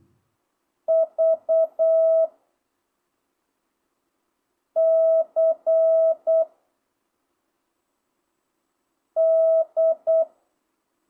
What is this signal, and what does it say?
Morse code, heard as a keyed tone, about 8 wpm, 660 Hz. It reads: V C D
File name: jh8cbh / 2025cw